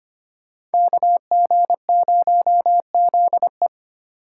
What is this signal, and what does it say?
Morse code, heard as a keyed tone, about 30 wpm, 700 Hz. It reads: KG0ZE
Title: Speed Racing - USA 2x2 Call Signs 25wpm